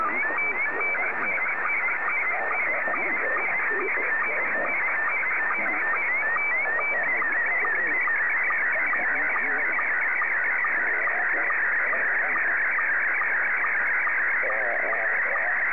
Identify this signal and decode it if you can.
SSTV (slow-scan television), Scottie 1, with speech.